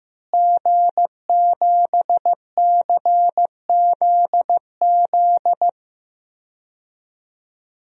Morse code, heard as a keyed tone, about 14 wpm, 700 Hz. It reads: G7CZZ
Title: DXCC-100 Call Signs - 5 Characters - Repeated 2x - 15wpm